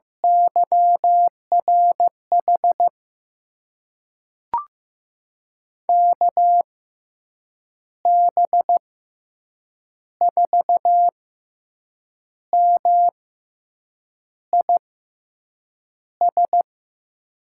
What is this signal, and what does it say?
Morse code, heard as a keyed tone, about 15 wpm, spaced as for 6 wpm, 700 Hz. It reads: YRH KB4MIS
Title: USA Call Signs - 6 Characters - 6x Character Spacing - 15wpm